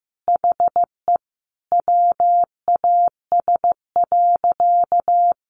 Morse code, heard as a keyed tone, about 14 wpm, 700 Hz. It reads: HE WAS.
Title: Sentences from Top 800 Words 15wpm